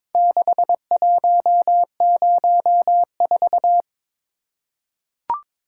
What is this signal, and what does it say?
Morse code, heard as a keyed tone, about 22 wpm, 700 Hz. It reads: 6104
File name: practice / 22wpm / Four Letter-Number Combinations - 22wpm